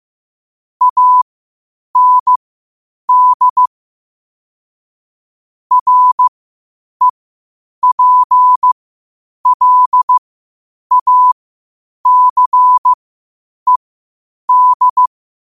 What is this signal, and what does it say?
Morse code, heard as a keyed tone, about 15 wpm, spaced as for 8 wpm, 1000 Hz. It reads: AND REPLACED
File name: qotdfiles / 8wpm